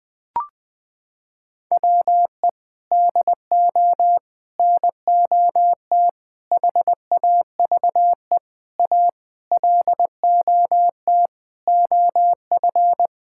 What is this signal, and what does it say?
Morse code, heard as a keyed tone, about 20 wpm, 700 Hz. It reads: WE DO NOT HAVE A LOT OF
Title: Sentences with 5-Word Phrases - 20wpm